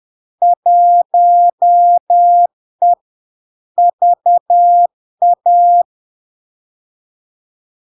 Morse code, heard as a keyed tone, about 10 wpm, 700 Hz. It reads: 1E VA
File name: LARK-FD-2022-Log-in-CW-10wpm-700-Hz